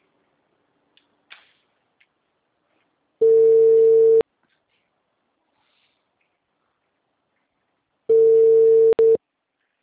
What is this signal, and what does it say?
Ringback tone (China, 450 Hz).
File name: out- -2016- - - .29567